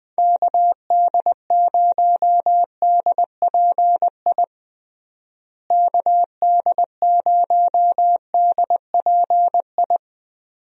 Morse code, heard as a keyed tone, about 20 wpm, 700 Hz. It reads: KD0DPI KD0DPI
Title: USA Call Signs - 6 Characters - Repeated 2x - 20wpm